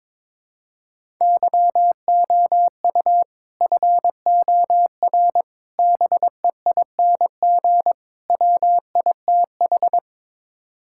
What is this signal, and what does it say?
Morse code, heard as a keyed tone, about 21 wpm, 700 Hz. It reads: YOU FOR BEING WITH
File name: practice / 22wpm / Build A Phrase - 5-Word Phrases - 22wpm